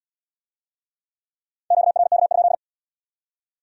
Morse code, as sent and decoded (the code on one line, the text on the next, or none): --.- .-. -.- ..--..
QRK?